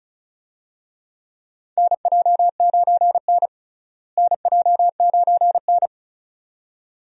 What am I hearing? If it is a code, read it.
Morse code, heard as a keyed tone, about 36 wpm, 700 Hz. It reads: NJ9N NJ9N
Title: USA Call Signs - 4 Characters - Repeated 2x - 35wpm